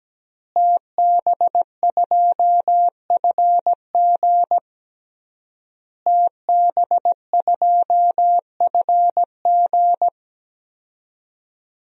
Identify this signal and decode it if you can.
Morse code, heard as a keyed tone, about 17 wpm, 700 Hz. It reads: TB2FG TB2FG
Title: DXCC-100 Call Signs - 5 Characters - Repeated 3x - 17wpm